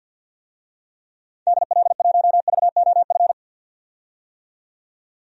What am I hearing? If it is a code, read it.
Morse code, heard as a keyed tone, about 47 wpm, 700 Hz. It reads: DL1VOF
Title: DXCC-100 Call Signs - 6 Characters - 50wpm